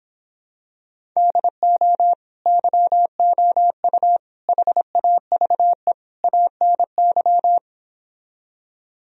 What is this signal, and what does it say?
Morse code, heard as a keyed tone, about 26 wpm, 700 Hz. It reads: DO YOU HAVE ANY